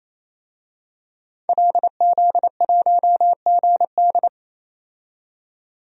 Morse code, heard as a keyed tone, about 28 wpm, 700 Hz. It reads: LZ1GD
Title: DXCC-100 Call Signs - 5 Characters - 28wpm